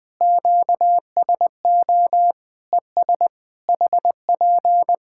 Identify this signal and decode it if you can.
Morse code, heard as a keyed tone, about 20 wpm, 700 Hz. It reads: QSO ES HP
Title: K1USN SST QSO Practice 20wpm